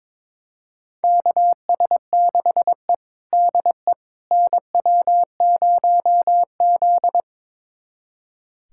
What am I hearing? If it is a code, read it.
Morse code, heard as a keyed tone, about 22 wpm, 700 Hz. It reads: KS6E DE NW0Z
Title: USA Callsign DE Callsign - 4 Characters - 22wpm